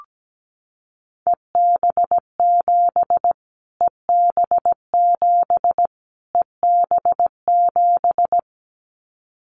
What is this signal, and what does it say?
Morse code, heard as a keyed tone, about 17 wpm, 700 Hz. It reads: EB7 EB7 EB7
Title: Callsign Prefixes for Top 25 DXCC - Repeated 3x - 17wpm